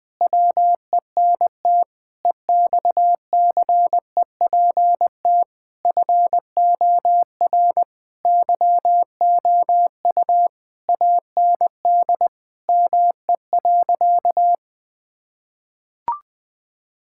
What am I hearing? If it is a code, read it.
Morse code, heard as a keyed tone, about 20 wpm, 700 Hz. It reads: WENT EXCEPT FOR YOU AND ME.
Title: Sentences from Top 1600 Words 20wpm